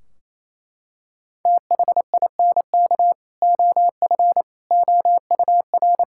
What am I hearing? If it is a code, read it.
Morse code, heard as a keyed tone, about 28 wpm, 700 Hz. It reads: THINK OF OUR
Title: Sentences from Top 1600 Words 28wpm